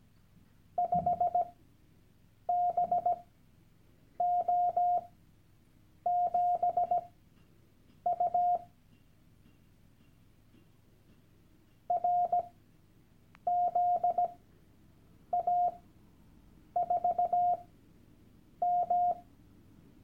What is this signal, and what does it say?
Morse code, heard as a keyed tone, about 17 wpm, spaced as for 7 wpm, 710 Hz. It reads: 5BO7U RZA4M
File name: Workshop Dateien / CW-Online-Kurs / Lektion 001